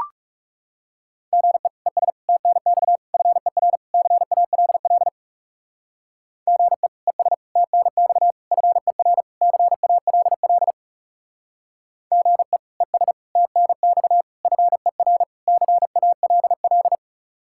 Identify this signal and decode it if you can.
Morse code, heard as a keyed tone, about 35 wpm, 700 Hz. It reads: GE ES TNX FER CALL GE ES TNX FER CALL GE ES TNX FER CALL
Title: Speed-Racing Common QSO Phrases - 30wpm